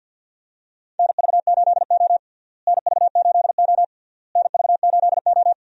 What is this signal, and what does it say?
Morse code, heard as a keyed tone, about 51 wpm, 700 Hz. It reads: NV8O NV8O NV8O
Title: USA Call Signs - 4 Characters - Repeated 3x - 50wpm